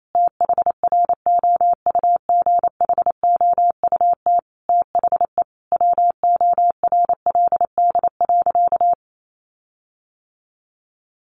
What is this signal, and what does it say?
Morse code, heard as a keyed tone, about 28 wpm, 700 Hz. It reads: THROUGHOUT THE WORLD.
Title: Sentences from Top 1500 Words 28wpm